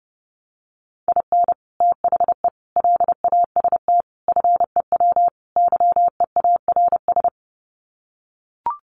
Morse code, heard as a keyed tone, about 30 wpm, 700 Hz. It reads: IN THE LAST FEW YEARS
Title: Sets of 5 Words 30wpm